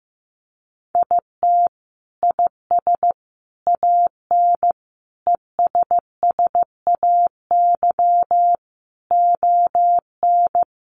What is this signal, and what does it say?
Morse code, heard as a keyed tone, about 15 wpm, 700 Hz. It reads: IT IS AN ESSAY ON
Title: Sentences from Top 1500 Words 15wpm